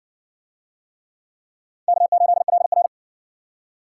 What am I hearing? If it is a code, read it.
Morse code, heard as a keyed tone, about 52 wpm, 700 Hz. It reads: K8LR